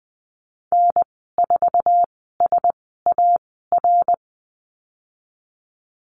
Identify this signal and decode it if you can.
Morse code, heard as a keyed tone, about 20 wpm, spaced as for 15 wpm, 700 Hz. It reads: N4SAR
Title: 069 Lesson 12g - Letter d - Call Signs 20wpm